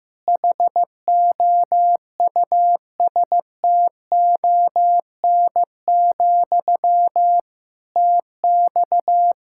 Morse code, heard as a keyed tone, about 15 wpm, 700 Hz. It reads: HOUSTON, TX
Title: K1USN SST QSO Practice 15wpm